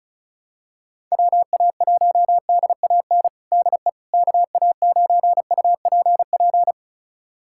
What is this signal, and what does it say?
Morse code, heard as a keyed tone, about 34 wpm, 700 Hz. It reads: WA1DAN DE KA9UPP
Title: USA Callsign DE Callsign 35wpm